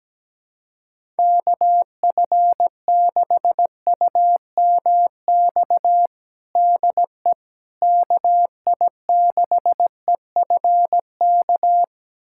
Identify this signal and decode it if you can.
Morse code, heard as a keyed tone, about 17 wpm, 700 Hz. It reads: KF6UMX DE KI6EFK
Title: USA Callsign DE Callsign - 6 Characters - 17wpm